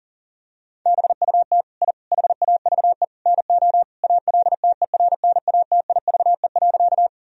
Morse code, heard as a keyed tone, about 40 wpm, 700 Hz. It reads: BUT I HAVE NO ALTERNATIVE.
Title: Sentences from Top 1000 Words - Review Part 4 - 40wpm